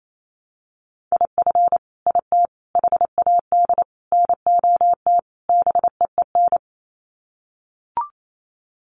Morse code, heard as a keyed tone, about 28 wpm, 700 Hz. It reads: IF IT HAD NOT BEEN
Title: Sets of 5 Words Encore 28wpm